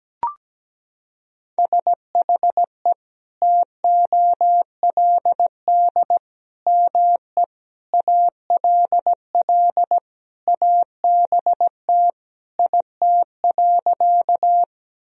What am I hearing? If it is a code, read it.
Morse code, heard as a keyed tone, about 17 wpm, 700 Hz. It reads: SHE TOLD ME ALL ABT IT.
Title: Sentences from Top 400 Words and Abbreviations 17wpm